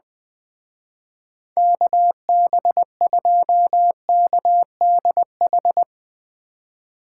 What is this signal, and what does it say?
Morse code, heard as a keyed tone, about 24 wpm, 700 Hz. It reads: KB2KDH
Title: Speed Racing - USA 2x3 Call Signs 20wpm